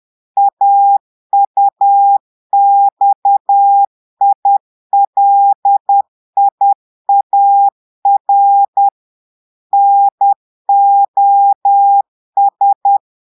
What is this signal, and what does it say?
Morse code, heard as a keyed tone, about 10 wpm, 800 Hz. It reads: AUXILIAR NOS
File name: Portugais - texte en clair 1 - vitesse - 10 mots minute